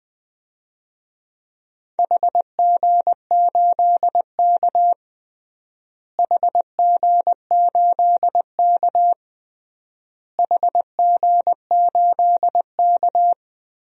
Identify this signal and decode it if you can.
Morse code, heard as a keyed tone, about 20 wpm, 700 Hz. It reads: HG8K HG8K HG8K
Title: DXCC-100 Call Signs - 4 Characters - Repeated 3x - 20wpm